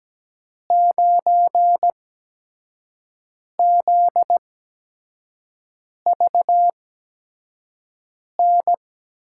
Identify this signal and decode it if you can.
Morse code, heard as a keyed tone, about 17 wpm, spaced as for 5 wpm, 700 Hz. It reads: 9ZVN
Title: USA Call Signs - 6 Characters - 8x Character Spacing - 17wpm